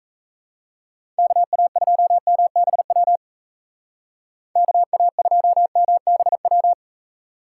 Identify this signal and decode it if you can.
Morse code, heard as a keyed tone, about 36 wpm, 700 Hz. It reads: KA2MBW KA2MBW